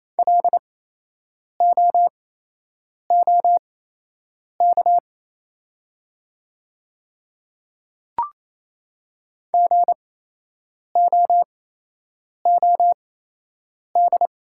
Morse code, heard as a keyed tone, about 28 wpm, spaced as for 8 wpm, 700 Hz. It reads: LOOK GOOD
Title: 8x Character Spacing - Top 200 Words - 28wpm